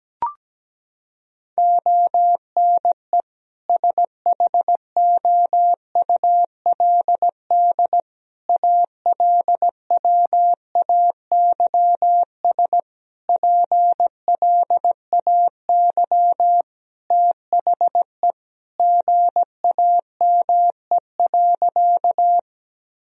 Morse code, heard as a keyed tone, about 17 wpm, 700 Hz. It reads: ONE SHOULD ALWAYS PLAY THE GAME.
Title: Sentences from Top 1000 Words - Review Part 3 - 17wpm